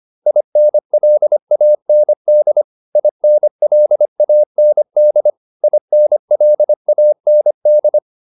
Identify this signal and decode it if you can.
Morse code, heard as a keyed tone, about 25 wpm, 600 Hz. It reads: INLAND INLAND INLAND